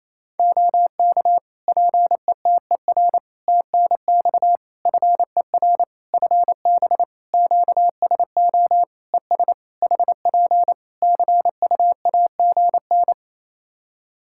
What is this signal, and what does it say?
Morse code, heard as a keyed tone, about 28 wpm, 700 Hz. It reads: OK PETER TNX FER FB QSO ES HP CUAGN
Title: Practice QSO - Exchange 4 - Ending - 28wpm